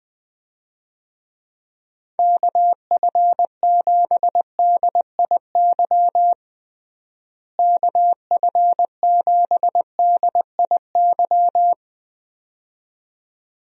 Morse code, heard as a keyed tone, about 21 wpm, 700 Hz. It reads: KF7DIY KF7DIY
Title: USA Call Signs - 6 Characters - Repeated 2x - 20wpm